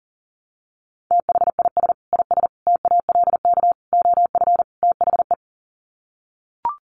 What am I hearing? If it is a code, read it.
Morse code, heard as a keyed tone, about 40 wpm, 700 Hz. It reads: THIS IS TALK OF THE
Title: Sets of 5 Words 40wpm